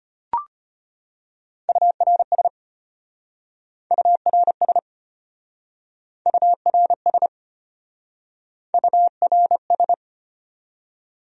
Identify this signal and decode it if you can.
Morse code, heard as a keyed tone, about 33 wpm, 700 Hz. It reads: URS URS URS URS